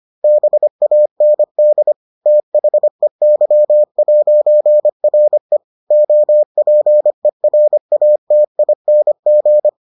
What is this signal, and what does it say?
Morse code, heard as a keyed tone, about 25 wpm, 600 Hz. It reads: BAND THEY'RE OPERATING